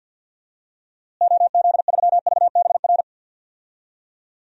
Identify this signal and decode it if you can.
Morse code, heard as a keyed tone, about 47 wpm, 700 Hz. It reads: OZ3VBR
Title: DXCC-100 Call Signs - 6 Characters - Repeated 2x - 50wpm